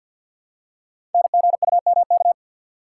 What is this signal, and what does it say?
Morse code, heard as a keyed tone, about 51 wpm, 700 Hz. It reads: NGUMK
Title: Code Groups - 5 Letters - 50wpm